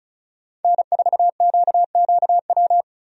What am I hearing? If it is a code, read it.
Morse code, heard as a keyed tone, about 36 wpm, 700 Hz. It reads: N4QQW
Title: USA Call Signs - 5 Characters - 35wpm